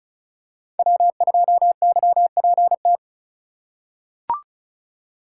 Morse code, heard as a keyed tone, about 36 wpm, 700 Hz. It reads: W2YPT
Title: USA Call Signs - 2x Character Spacing - 35wpm